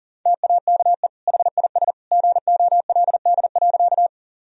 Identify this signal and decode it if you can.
Morse code, heard as a keyed tone, about 40 wpm, 700 Hz. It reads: TAKE HIS GOLD.
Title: Sentences from Top 1000 Words - Review Part 3 - 40wpm